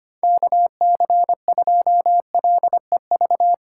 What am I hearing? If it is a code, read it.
Morse code, heard as a keyed tone, about 25 wpm, 700 Hz. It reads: KC2LEV
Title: USA Call Signs - 6 Characters - 8x Character Spacing - 25wpm